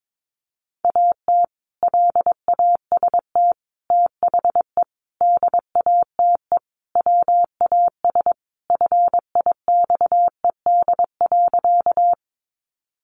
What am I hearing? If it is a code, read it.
Morse code, heard as a keyed tone, about 22 wpm, 700 Hz. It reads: AT LAST THE DATE WAS FIXED.
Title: Sentences from Top 1100 Words 22wpm